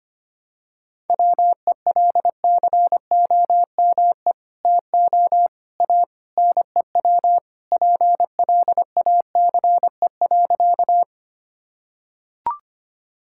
Morse code, heard as a keyed tone, about 25 wpm, 700 Hz. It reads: WELCOME TO A NEW PLACE.
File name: practice / 25wpm / Sentences with 2-Word Phrases - 25wpm